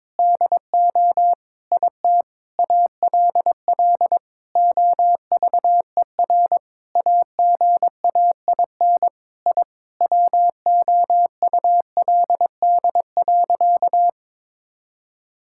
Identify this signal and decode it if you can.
Morse code, heard as a keyed tone, about 22 wpm, 700 Hz. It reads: DO IT ALL OVER AGAIN I WOULD.
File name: Sentences with 9-Word Phrases - 22wpm